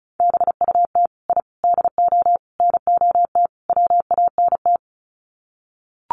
Morse code, heard as a keyed tone, about 34 wpm, 700 Hz. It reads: BUT I DO NOT WANT
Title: Sets of 5 Words 35wpm